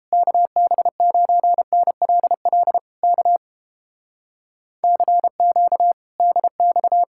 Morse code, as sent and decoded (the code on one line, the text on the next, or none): -.- -... ----. -. .-.. .-.. / -.- / -.-. --.- / -.. -..-
KB9NLL K CQ DX